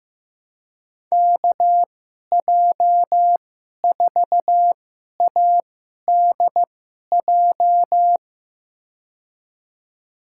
Morse code, heard as a keyed tone, about 15 wpm, spaced as for 11 wpm, 700 Hz. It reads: KJ4ADJ
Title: USA Call Signs - 6 Characters - 2x Character Spacing - 15wpm